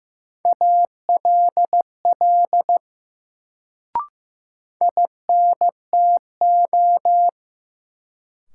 Morse code, heard as a keyed tone, about 16 wpm, 700 Hz. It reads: ALL INTO
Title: Sets of 2 Words - New Word Drill - 15wpm